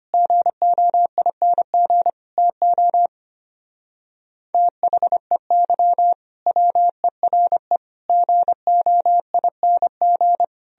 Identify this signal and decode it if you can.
Morse code, heard as a keyed tone, about 29 wpm, 700 Hz. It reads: GOING TO THEY WERE GOING
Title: Speed-Racing - Sets of 4 Words 25wpm